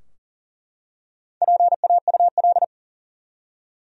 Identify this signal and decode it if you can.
Morse code, heard as a keyed tone, about 40 wpm, 700 Hz. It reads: PAUL
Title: USA Common Names - Max 5 Letters - 40wpm